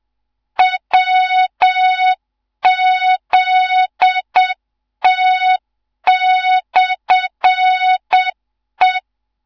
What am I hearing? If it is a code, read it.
Morse code, heard as a keyed tone, about 7 wpm, 3700 Hz, with noise. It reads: WZT/E
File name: Morze-HA2MN / hang / 15 GYAKORLAT